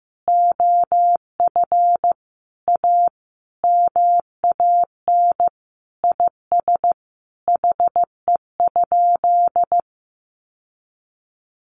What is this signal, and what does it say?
Morse code, heard as a keyed tone, about 15 wpm, 700 Hz. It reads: OF A MAN IS HE?
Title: Sentences from Top 1300 Words 15wpm